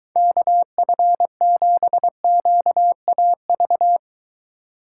Morse code, heard as a keyed tone, about 22 wpm, 700 Hz. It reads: KF7QAV